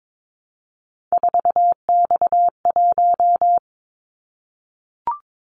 Morse code, heard as a keyed tone, about 22 wpm, 700 Hz. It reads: 4X1
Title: Callsign Prefixes for Top 50 DXCC - 22wpm